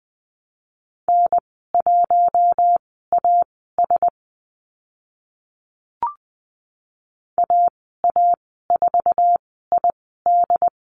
Morse code, heard as a keyed tone, about 20 wpm, spaced as for 14 wpm, 700 Hz. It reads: N1AS AA4ID
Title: 069 Lesson 12g - Letter d - Call Signs 20wpm